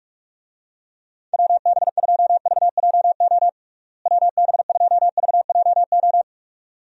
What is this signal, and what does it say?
Morse code, heard as a keyed tone, about 42 wpm, 700 Hz. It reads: WB2VJO WB2VJO
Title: USA Call Signs - 6 Characters - Repeated 2x - 45wpm